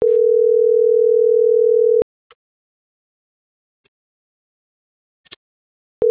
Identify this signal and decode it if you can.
Ringback tone (North America), 440 + 480 Hz.
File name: CENTRAL SAGOT / out isai.hernandez